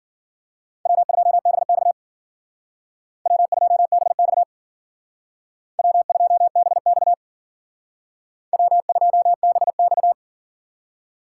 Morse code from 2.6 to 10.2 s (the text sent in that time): W2BX W2BX W2BX